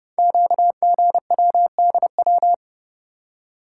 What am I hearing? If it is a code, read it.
Morse code, heard as a keyed tone, about 30 wpm, 700 Hz. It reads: QGWDW
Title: Code Groups - 5 Letters - 30wpm